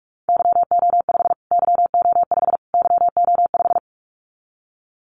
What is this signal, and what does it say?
Morse code, heard as a keyed tone, about 46 wpm, 700 Hz. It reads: YO5 YO5 YO5
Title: Callsign Prefixes for Top 25 DXCC - Repeated 3x - 45wpm